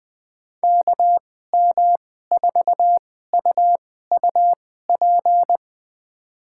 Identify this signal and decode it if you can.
Morse code, heard as a keyed tone, about 20 wpm, spaced as for 14 wpm, 700 Hz. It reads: KM4UUP
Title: USA Call Signs - 6 Characters - 2x Character Spacing - 20wpm